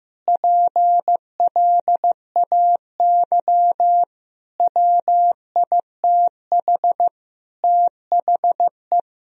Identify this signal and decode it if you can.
Morse code, heard as a keyed tone, about 16 wpm, 700 Hz. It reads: PLAY WITH THE
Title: Sentences from Top 1000 Words - Review Part 2 - 15wpm